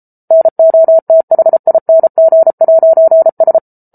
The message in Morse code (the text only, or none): NOTHING'S